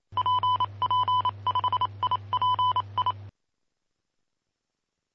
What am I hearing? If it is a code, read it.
Morse code, heard as a keyed tone, about 28 wpm, spaced as for 25 wpm, 1000 Hz. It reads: PP5IPI
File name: pp5bar / aulas de telegrafia / Aula-16